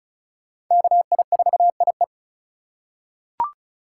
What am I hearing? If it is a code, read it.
Morse code, heard as a keyed tone, about 36 wpm, 700 Hz. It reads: KI4IE
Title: USA Call Signs - 5 Characters - 3x Character Spacing - 35wpm